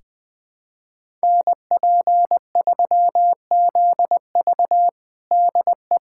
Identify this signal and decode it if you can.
Morse code, heard as a keyed tone, about 20 wpm, 700 Hz. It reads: NP3ZV DE
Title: USA Callsign DE Callsign - 5 Characters - 20wpm